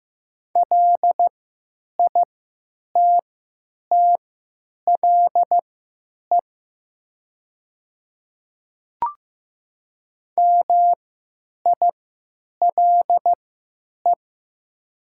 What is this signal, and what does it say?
Morse code, heard as a keyed tone, about 15 wpm, spaced as for 9 wpm, 700 Hz. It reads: LITTLE MILE
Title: 3x Character Spacing - Top 300 Words - 15wpm